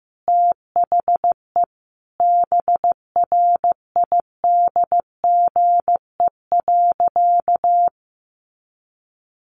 Morse code, heard as a keyed tone, about 15 wpm, 700 Hz. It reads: THE BRIDGE.
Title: Sentences from Top 1500 Words 15wpm